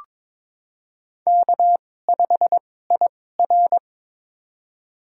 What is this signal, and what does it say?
Morse code, heard as a keyed tone, about 22 wpm, spaced as for 16 wpm, 700 Hz. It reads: K5IR
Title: USA Call Signs - 4 Characters - 2x Character Spacing - 22wpm